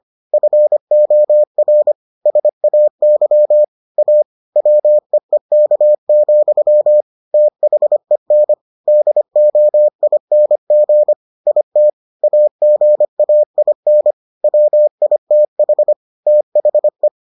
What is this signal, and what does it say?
Morse code, heard as a keyed tone, about 25 wpm, 600 Hz. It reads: FOR SAY A WEEK, THEN DOING IT AGAIN WITH THE